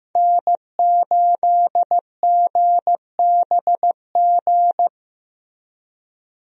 Morse code, heard as a keyed tone, about 15 wpm, 700 Hz. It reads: N8GBG
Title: USA Call Signs - 5 Characters - Repeated 3x - 15wpm